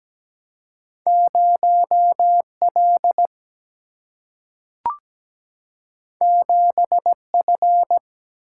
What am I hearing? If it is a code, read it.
Morse code, heard as a keyed tone, about 17 wpm, 700 Hz. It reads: 0 L 7 F
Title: Single-Letter Single-Number 17wpm